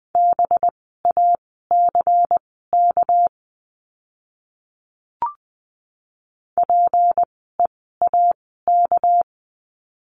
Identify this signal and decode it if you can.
Morse code, heard as a keyed tone, about 20 wpm, spaced as for 14 wpm, 700 Hz. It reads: BACK PEAK
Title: 199 Lesson 32f - Letter k - Words 20wpm